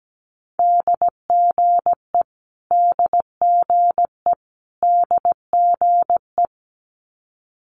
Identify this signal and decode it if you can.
Morse code, heard as a keyed tone, about 17 wpm, 700 Hz. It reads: DGE DGE DGE
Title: Multi-letter Phonemes Repeated 3x 17wpm